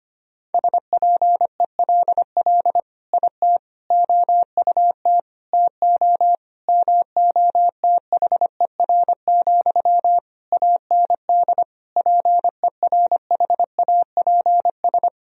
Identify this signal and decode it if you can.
Morse code, heard as a keyed tone, about 25 wpm, 700 Hz. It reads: SPELL IT OUT TO MOTHER, AND PERHAPS
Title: Sentences from Top 1000 Words - Review Part 1 - 25wpm